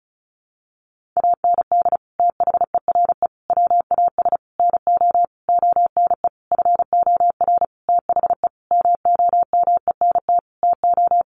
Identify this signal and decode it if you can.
Morse code, heard as a keyed tone, about 34 wpm, 700 Hz. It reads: AND THERE WAS NO ONE FOR THE MOMENT TO
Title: Sentences from Top 1500 Words 35wpm